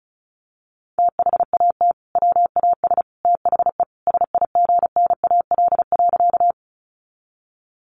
Morse code, heard as a keyed tone, about 34 wpm, 700 Hz. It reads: THAT WAS THE SIGNAL.
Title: Sentences from Top 1400 Words 35wpm